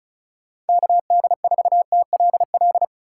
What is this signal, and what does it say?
Morse code, heard as a keyed tone, about 36 wpm, 700 Hz. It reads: KD4TLL
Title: USA Call Signs - 6 Characters - 2x Character Spacing - 35wpm